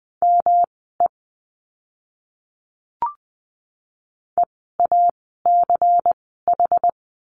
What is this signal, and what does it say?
Morse code, heard as a keyed tone, about 20 wpm, spaced as for 14 wpm, 700 Hz. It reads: ME EACH
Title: 192 Lesson 31f - Letter v - Words 20wpm